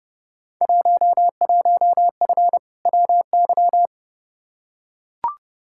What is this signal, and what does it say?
Morse code, heard as a keyed tone, about 31 wpm, 700 Hz. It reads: 11F WY
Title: Field Day Exchanges Encore - 30wpm